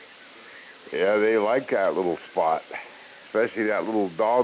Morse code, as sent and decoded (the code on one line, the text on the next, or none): none